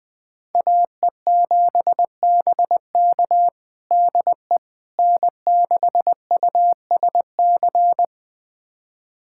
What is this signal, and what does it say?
Morse code, heard as a keyed tone, about 20 wpm, 700 Hz. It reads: AE7BK DE N6USC
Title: USA Callsign DE Callsign - 5 Characters - 20wpm